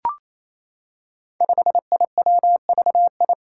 Morse code, heard as a keyed tone, about 28 wpm, 700 Hz. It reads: 5IWVI
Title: Five Challenging Letter-Number Combinations 28wpm